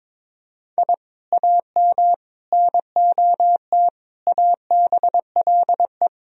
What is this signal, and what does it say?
Morse code, heard as a keyed tone, about 21 wpm, 700 Hz. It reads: I AM NOT ABLE
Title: Sentences from Top 1000 Words - Review Part 2 - 22wpm